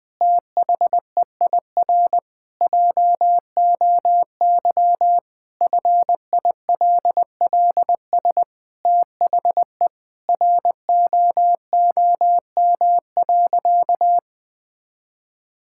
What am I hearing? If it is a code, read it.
Morse code, heard as a keyed tone, about 20 wpm, 700 Hz. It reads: THEIR JOY FILLS THE ROOM.
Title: Sentences with 2-Word Phrases - 20wpm